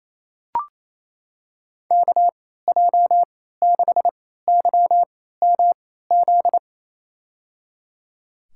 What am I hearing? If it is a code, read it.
Morse code, heard as a keyed tone, about 27 wpm, spaced as for 16 wpm, 700 Hz. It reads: KJ6YMZ
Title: USA Call Signs - 3x Character Spacing - 28wpm